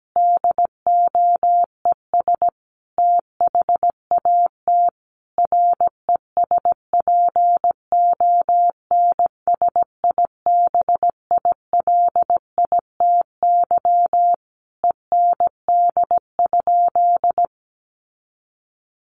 Morse code, heard as a keyed tone, about 17 wpm, 700 Hz. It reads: DOES THAT RESPONSIBILITY END?
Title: Sentences from Top 1200 Words 17wpm